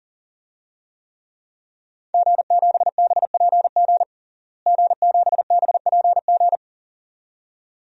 Morse code, heard as a keyed tone, about 40 wpm, 700 Hz. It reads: G7BPG G7BPG
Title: DXCC-100 Call Signs - 5 Characters - Repeated 2x - 40wpm